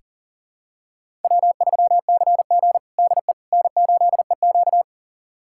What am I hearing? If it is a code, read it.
Morse code, heard as a keyed tone, about 40 wpm, 700 Hz. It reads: W3CG DE N8EQ
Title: USA Callsign DE Callsign - 4 Characters - 40wpm